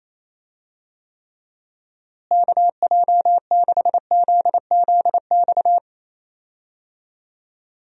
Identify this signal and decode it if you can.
Morse code, heard as a keyed tone, about 28 wpm, 700 Hz. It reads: KJ6ZZX
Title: USA Call Signs - 6 Characters - Repeated 2x - 28wpm